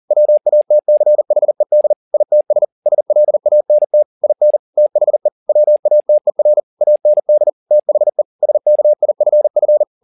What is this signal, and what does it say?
Morse code, heard as a keyed tone, about 40 wpm, 600 Hz. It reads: WATCHED ITS SLANT IN THE WATER AND THE SKIFF